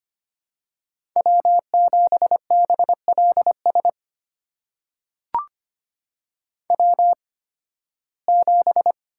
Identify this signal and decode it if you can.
Morse code, heard as a keyed tone, about 25 wpm, spaced as for 16 wpm, 700 Hz. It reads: W7BLS W7
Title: USA Call Signs - 8x Character Spacing - 25wpm